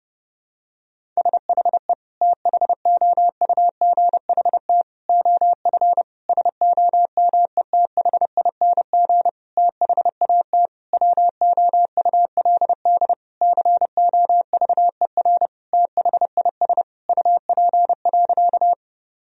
Morse code, heard as a keyed tone, about 30 wpm, 700 Hz. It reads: SHE THOUGHT OF SOMETHING THAT WOULD COVER THIS UP.
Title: Sentences from Top 1000 Words - Review Part 1 - 30wpm